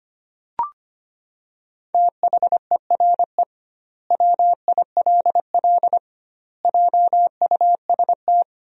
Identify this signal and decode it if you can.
Morse code, heard as a keyed tone, about 25 wpm, 700 Hz. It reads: THERE WILL JUST
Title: 2x Word-Spacing - Sentences from Top 200 Words - 25wpm